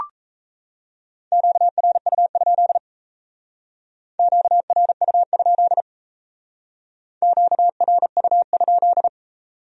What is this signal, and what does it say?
Morse code, heard as a keyed tone, about 35 wpm, 700 Hz. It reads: QRU? QRU? QRU?
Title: Speed-Racing Q-Codes - 28wpm